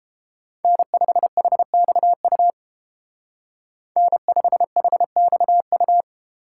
Morse code, heard as a keyed tone, about 30 wpm, 700 Hz. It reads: N5HXU N5HXU